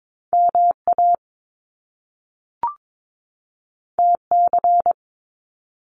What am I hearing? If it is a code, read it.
Morse code, heard as a keyed tone, about 22 wpm, 700 Hz. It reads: MA TC
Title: Top Two Letter Combinations in Words 22wpm